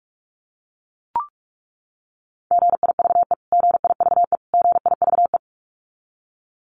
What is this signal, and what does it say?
Morse code, heard as a keyed tone, about 44 wpm, 700 Hz. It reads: GIVE GIVE GIVE
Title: Top 200 Words Repeated 45wpm